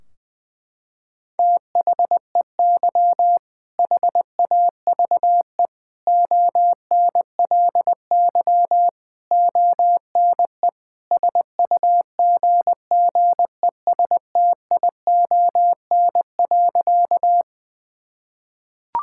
Morse code, heard as a keyed tone, about 20 wpm, 700 Hz. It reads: THEY HAVE ONLY ONE SUGGESTION.
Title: Sentences from Top 1600 Words 20wpm